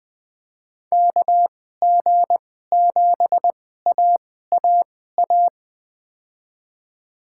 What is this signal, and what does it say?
Morse code, heard as a keyed tone, about 21 wpm, spaced as for 15 wpm, 700 Hz. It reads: KG7AAA
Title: USA Call Signs - 6 Characters - 2x Character Spacing - 20wpm